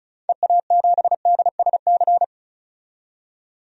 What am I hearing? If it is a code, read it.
Morse code, heard as a keyed tone, about 34 wpm, 700 Hz. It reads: EA7DSC
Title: DXCC-100 Call Signs - 6 Characters - 35wpm